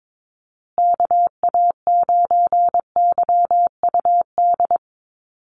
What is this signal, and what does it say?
Morse code, heard as a keyed tone, about 22 wpm, 700 Hz. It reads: KA9YUD
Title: USA Call Signs Repeated 3x 22wpm